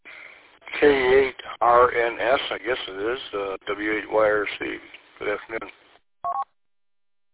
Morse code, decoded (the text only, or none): none